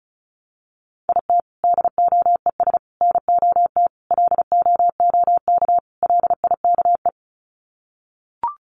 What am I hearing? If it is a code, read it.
Morse code, heard as a keyed tone, about 34 wpm, 700 Hz. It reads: IT DOES NOT LOOK LIKE
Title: Sets of 5 Words Encore2 35wpm